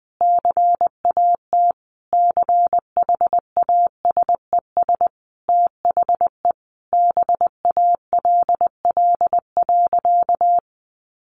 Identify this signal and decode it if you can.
Morse code, heard as a keyed tone, about 20 wpm, 700 Hz. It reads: CAT CHASES THE BALL.
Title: Additional Easy Phrases 20wpm